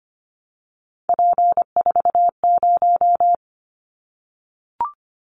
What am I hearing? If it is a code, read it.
Morse code, heard as a keyed tone, about 24 wpm, 700 Hz. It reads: P40
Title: Callsign Prefixes for Top 100 DXCC - 25wpm